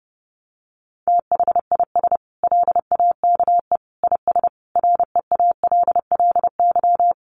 Morse code, heard as a keyed tone, about 28 wpm, 700 Hz. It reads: THIS LAKE IS REALLY
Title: Sentences from Top 1100 Words 30wpm